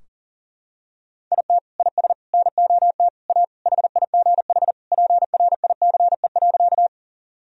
Morse code, heard as a keyed tone, about 40 wpm, 700 Hz. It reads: IT IS NOT A HIGH PRICE.
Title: Sentences from Top 1000 Words - Review Part 4 - 40wpm